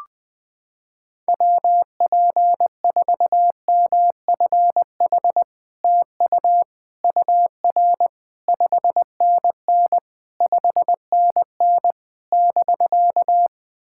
Morse code, heard as a keyed tone, about 21 wpm, 700 Hz. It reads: WP4MFH TU UR 5NN 5NN <BK>
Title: SOTA QSO Practice 20wpm